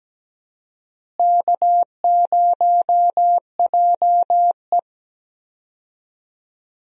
Morse code, heard as a keyed tone, about 17 wpm, 700 Hz. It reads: K0JE
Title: USA Call Signs - 4 Characters - Repeated 2x - 17wpm